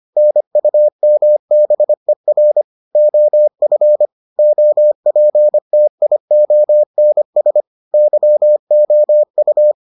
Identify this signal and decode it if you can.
Morse code, heard as a keyed tone, about 25 wpm, 600 Hz. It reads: NUMBER OF OPTIONS YOU